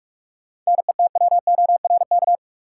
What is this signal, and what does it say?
Morse code, heard as a keyed tone, about 43 wpm, 700 Hz. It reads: NETWORK
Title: Sentences from Top 1000 Words - Review Part 7 - New Word Drill - 45wpm